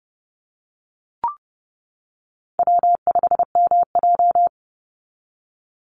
Morse code, heard as a keyed tone, about 30 wpm, 700 Hz. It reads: W5MJ
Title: Easy Call Signs - 30wpm